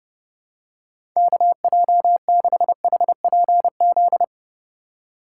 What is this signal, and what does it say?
Morse code, heard as a keyed tone, about 32 wpm, 700 Hz. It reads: KJ6HPZ
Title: USA Call Signs - 4x Character Spacing - 30wpm